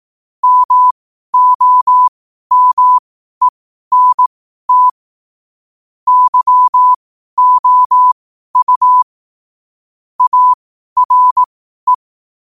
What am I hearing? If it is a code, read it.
Morse code, heard as a keyed tone, about 18 wpm, spaced as for 12 wpm, 1000 Hz. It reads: MOMENT YOU ARE
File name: qotdfiles / 12wpm